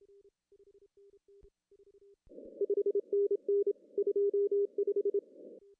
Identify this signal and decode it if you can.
Morse code, heard as a keyed tone, about 25 wpm, 400 Hz. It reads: 5NN 25
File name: CW / cqww-cw-k5tr